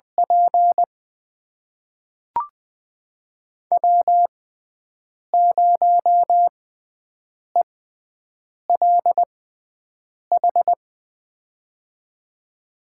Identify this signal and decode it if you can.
Morse code, heard as a keyed tone, about 20 wpm, spaced as for 7 wpm, 700 Hz. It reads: P W0ELH
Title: USA Call Signs - 6x Character Spacing - 20wpm